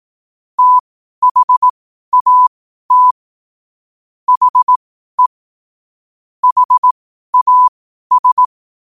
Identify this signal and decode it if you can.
Morse code, heard as a keyed tone, about 18 wpm, spaced as for 12 wpm, 1000 Hz. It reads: THAT HE HAS
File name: qotdfiles / 12wpm